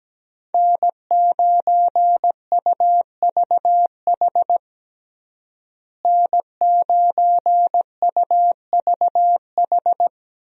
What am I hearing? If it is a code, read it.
Morse code, heard as a keyed tone, about 17 wpm, 700 Hz. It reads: N9UVH N9UVH
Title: USA Call Signs - 5 Characters - Repeated 3x - 17wpm